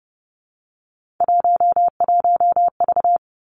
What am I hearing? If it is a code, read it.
Morse code, heard as a keyed tone, about 31 wpm, 700 Hz. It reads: 11V
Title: Three Challenging Letter-Number Combinations 30wpm